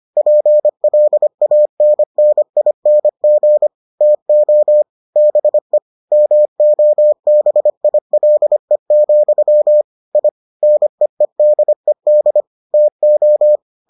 Morse code, heard as a keyed tone, about 25 wpm, 600 Hz. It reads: PLANNING TO BE MOBILE, I NEEDED TO